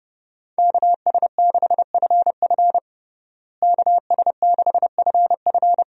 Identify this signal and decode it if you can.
Morse code, heard as a keyed tone, about 30 wpm, 700 Hz. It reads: KS6FF KS6FF
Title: USA Call Signs - 5 Characters - Repeated 2x - 30wpm